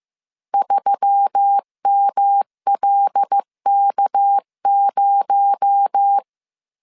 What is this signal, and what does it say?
Morse code, heard as a keyed tone, about 15 wpm, 800 Hz. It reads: 3MLK0